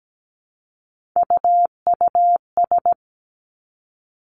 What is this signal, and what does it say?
Morse code, heard as a keyed tone, about 17 wpm, 700 Hz. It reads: UUS
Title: Three Challenging Letter-Number Combinations 17wpm